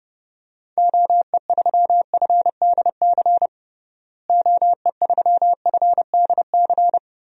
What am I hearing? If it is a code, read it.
Morse code, heard as a keyed tone, about 29 wpm, 700 Hz. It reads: OE3FDC OE3FDC
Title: DXCC-100 Call Signs - 6 Characters - Repeated 2x - 30wpm